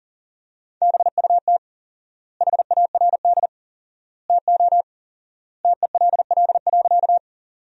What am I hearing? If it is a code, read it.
Morse code, heard as a keyed tone, about 40 wpm, 700 Hz. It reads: BUT HARD TO TELL.
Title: 4x Word-Spacing - Sentences from Top 300 Words - 40wpm